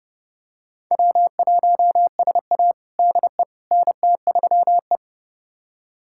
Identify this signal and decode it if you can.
Morse code, heard as a keyed tone, about 30 wpm, 700 Hz. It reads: W1SA DE NT3E
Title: USA Callsign DE Callsign - 4 Characters - 30wpm